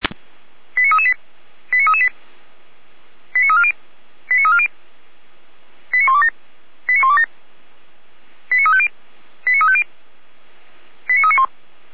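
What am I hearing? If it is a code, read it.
Five-tone selective call (ZVEI), heard as quick runs of five tones, tones of about 70 ms. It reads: 89338 89338 89340 89340 89127 89127 89350 89350 89391
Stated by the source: ZVEI five-tone selective call received by SDR, from the Feuerwehr Orlamünde fire brigade audio archive.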